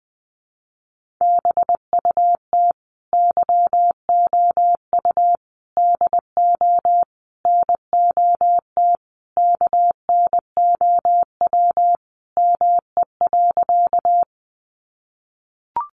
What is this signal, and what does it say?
Morse code, heard as a keyed tone, about 20 wpm, 700 Hz. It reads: BUT YOU DO NOT KNOW ME.
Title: Sentences from Top 600 Words 20wpm